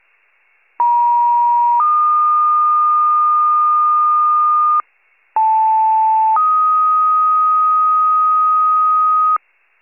Two-tone page, heard as a one-second tone, then a three-second tone, about 950 then 1220 Hz, about 880 then 1250 Hz. It is heard from 0.8 to 9.4 s.
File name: pager / 76 FIRE / wcalert